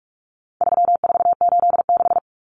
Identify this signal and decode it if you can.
Morse code, heard as a keyed tone, about 44 wpm, 700 Hz. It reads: 3486
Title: Common Cut Numbers - 4-Digit 45wpm